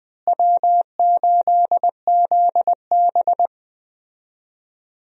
Morse code, heard as a keyed tone, about 21 wpm, 700 Hz. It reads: W8ZB
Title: USA Call Signs - 4 Characters - Repeated 3x - 20wpm